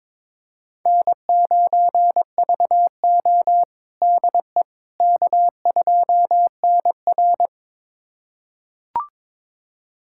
Morse code, heard as a keyed tone, about 22 wpm, 700 Hz. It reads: N9VO DE K2NR
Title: USA Callsign DE Callsign - 4 Characters - 22wpm